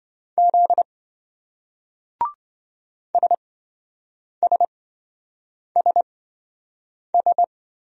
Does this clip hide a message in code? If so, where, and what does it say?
Yes: Morse code from 0.4 to 7.4 s, heard as a keyed tone, about 28 wpm, spaced as for 7 wpm, 700 Hz. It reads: Z SSSS